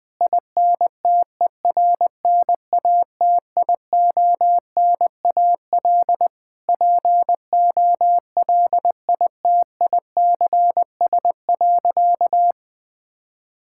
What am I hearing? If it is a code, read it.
Morse code, heard as a keyed tone, about 20 wpm, 700 Hz. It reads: INTERNATIONAL POLITICS.
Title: Sentences from Top 1000 Words - Review Part 6 - 20wpm